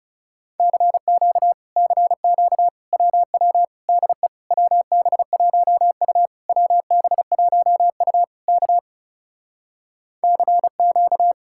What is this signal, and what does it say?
Morse code, heard as a keyed tone, about 40 wpm, 700 Hz. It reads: CQ CQ WW DE WB1U WB1U K CQ